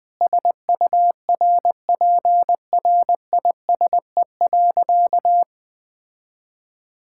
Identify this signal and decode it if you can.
Morse code, heard as a keyed tone, about 21 wpm, 700 Hz. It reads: SURPRISE.
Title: Sentences with 5-Word Phrases - 20wpm